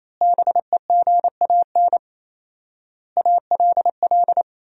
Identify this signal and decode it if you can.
Morse code, heard as a keyed tone, about 28 wpm, 700 Hz. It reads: BEGAN ALL
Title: 4x Word-Spacing - Sentences from Top 500 Words - 28wpm